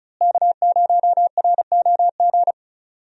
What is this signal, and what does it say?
Morse code, heard as a keyed tone, about 36 wpm, 700 Hz. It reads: K0ROG
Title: USA Call Signs - 5 Characters - Repeated 3x - 35wpm